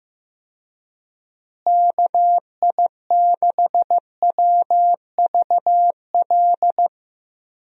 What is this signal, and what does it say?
Morse code, heard as a keyed tone, about 15 wpm, 700 Hz. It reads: KI6WVL
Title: USA Call Signs - 6 Characters - Repeated 3x - 15wpm